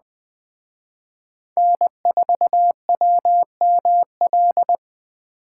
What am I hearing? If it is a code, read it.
Morse code, heard as a keyed tone, about 20 wpm, 700 Hz. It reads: N4WML